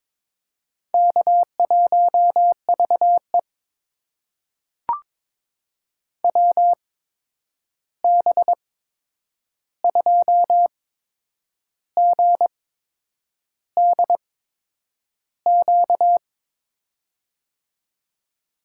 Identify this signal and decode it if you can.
Morse code, heard as a keyed tone, about 22 wpm, spaced as for 8 wpm, 700 Hz. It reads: K1VE WB2GDQ
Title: USA Call Signs - 8x Character Spacing - 22wpm